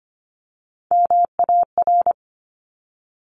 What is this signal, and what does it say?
Morse code, heard as a keyed tone, about 25 wpm, 700 Hz. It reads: MAR
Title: Top Three Letter Combinations in Words 25wpm